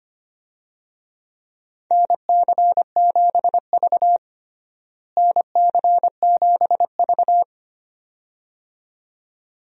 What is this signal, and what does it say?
Morse code, heard as a keyed tone, about 25 wpm, 700 Hz. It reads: NC7V NC7V
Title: USA Call Signs - 4 Characters - Repeated 2x - 25wpm